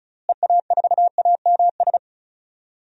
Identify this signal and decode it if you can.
Morse code, heard as a keyed tone, about 36 wpm, 700 Hz. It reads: EA4AMS
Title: DXCC-100 Call Signs - 6 Characters - Repeated 3x - 35wpm